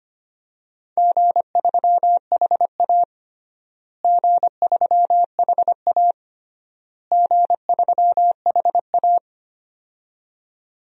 Morse code, heard as a keyed tone, about 25 wpm, 700 Hz. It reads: G3HA G3HA G3HA
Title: DXCC-100 Call Signs - 4 Characters - Repeated 3x - 25wpm